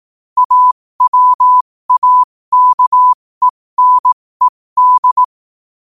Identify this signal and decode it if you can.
Morse code, heard as a keyed tone, about 18 wpm, spaced as for 15 wpm, 1000 Hz. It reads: AWAKENED
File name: qotdfiles / 15wpm